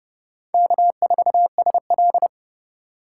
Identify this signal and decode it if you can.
Morse code, heard as a keyed tone, about 28 wpm, 700 Hz. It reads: K4SL